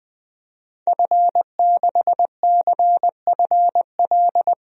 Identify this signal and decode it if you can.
Morse code, heard as a keyed tone, about 21 wpm, 700 Hz. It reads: F6CFL
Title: DXCC-100 Call Signs - 5 Characters - Repeated 2x - 20wpm